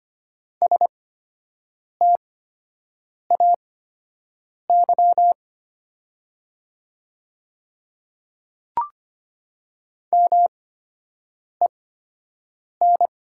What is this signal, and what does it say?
Morse code, heard as a keyed tone, about 25 wpm, spaced as for 7 wpm, 700 Hz. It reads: STAY MEN
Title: 8x Character Spacing - Top 500 Words - 25wpm